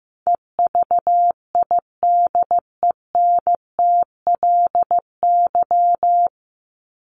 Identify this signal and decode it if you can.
Morse code, heard as a keyed tone, about 15 wpm, 700 Hz. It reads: EVIDENTLY
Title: LY Suffix - Top 100 Words with Base Words 15wpm